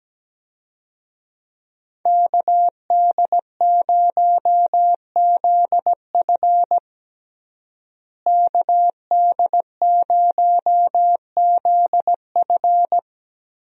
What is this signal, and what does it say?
Morse code, heard as a keyed tone, about 17 wpm, 700 Hz. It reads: KD0ZF KD0ZF
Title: USA Call Signs - 5 Characters - Repeated 3x - 17wpm